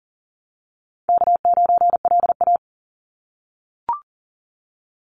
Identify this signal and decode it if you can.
Morse code, heard as a keyed tone, about 37 wpm, 700 Hz. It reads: K9LA
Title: Call Signs - 40wpm